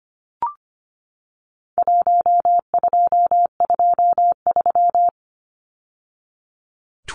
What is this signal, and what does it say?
Morse code, heard as a keyed tone, about 25 wpm, 700 Hz. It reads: 1223
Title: Numbers - Contest Style - 4-Digit 25wpm